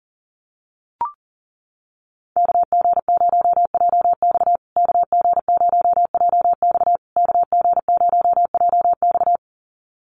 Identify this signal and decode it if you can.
Morse code, heard as a keyed tone, about 40 wpm, 700 Hz. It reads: KG0JX KG0JX KG0JX
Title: USA Call Signs Repeated 3x 40wpm